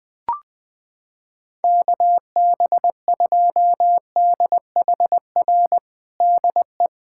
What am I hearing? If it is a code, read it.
Morse code, heard as a keyed tone, about 20 wpm, 700 Hz. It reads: KB2DHR DE
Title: USA Callsign DE Callsign - 6 Characters - 20wpm